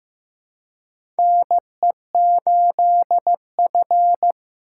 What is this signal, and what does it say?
Morse code, heard as a keyed tone, about 15 wpm, 700 Hz. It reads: NE8F
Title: USA Call Signs - 4 Characters - 4x Character Spacing - 15wpm